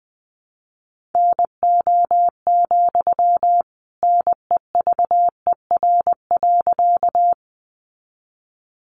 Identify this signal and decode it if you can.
Morse code, heard as a keyed tone, about 20 wpm, 700 Hz. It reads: NO, NEVER.
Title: Sentences from Top 300 Words 20wpm